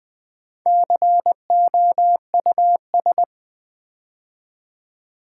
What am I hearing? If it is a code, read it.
Morse code, heard as a keyed tone, about 20 wpm, 700 Hz. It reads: COUS
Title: Build A Word - 2 Letters At A Time - 6-Letter Words - 20wpm